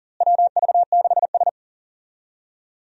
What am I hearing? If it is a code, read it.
Morse code, heard as a keyed tone, about 39 wpm, 700 Hz. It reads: WV6S
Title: USA Call Signs - 4 Characters - Repeated 3x - 40wpm